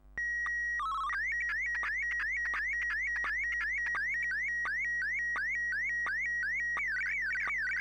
SSTV (slow-scan television), PD 90.